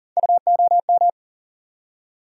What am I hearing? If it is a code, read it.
Morse code, heard as a keyed tone, about 40 wpm, 700 Hz. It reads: UOM
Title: USA Call Sign Postfixes - 3 Letters 40wpm